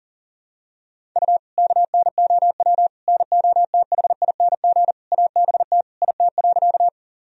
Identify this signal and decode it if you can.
Morse code, heard as a keyed tone, about 40 wpm, 700 Hz. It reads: U KNOW NOTHING ABT IT.
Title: Sentences from Top 500 Words and Abbreviations 40wpm